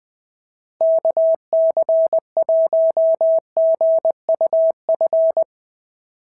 Morse code, heard as a keyed tone, about 20 wpm, 650 Hz. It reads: KC1GUF
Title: SOTA QSO Practice 20wpm